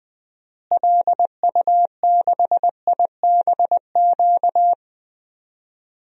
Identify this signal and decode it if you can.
Morse code, heard as a keyed tone, about 21 wpm, 700 Hz. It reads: LU6IBQ
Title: DXCC-100 Call Signs - 6 Characters - Repeated 2x - 20wpm